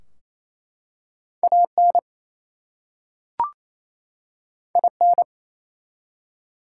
Morse code, heard as a keyed tone, about 27 wpm, 700 Hz. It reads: AN IN
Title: Two Letter Words - 28wpm